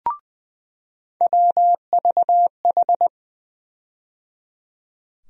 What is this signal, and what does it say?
Morse code, heard as a keyed tone, about 20 wpm, 700 Hz. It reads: WVH
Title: USA Call Sign Postfixes - 3 Letters 20wpm